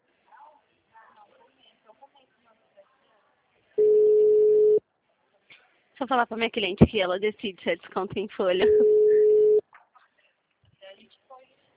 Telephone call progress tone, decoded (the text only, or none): ringback tone